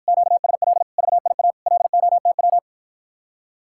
Morse code, heard as a keyed tone, about 53 wpm, 700 Hz. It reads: QSL VIA LOTW